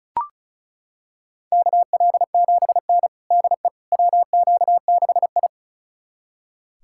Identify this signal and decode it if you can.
Morse code, heard as a keyed tone, about 36 wpm, 700 Hz. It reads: KL7N DE WQ6I
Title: USA Callsign DE Callsign - 4 Characters - 35wpm